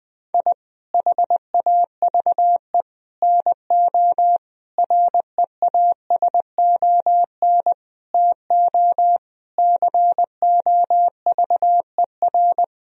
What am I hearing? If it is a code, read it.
Morse code, heard as a keyed tone, about 20 wpm, 700 Hz. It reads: I HAVE NO REASON TO COVER
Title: Sentences from Top 1000 Words - Review Part 1 - 20wpm